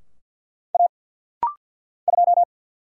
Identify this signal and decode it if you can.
Morse code, heard as a keyed tone, about 50 wpm, 700 Hz. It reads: A 2
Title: Single Letter-Number Alt - Rapid-Fire 50wpm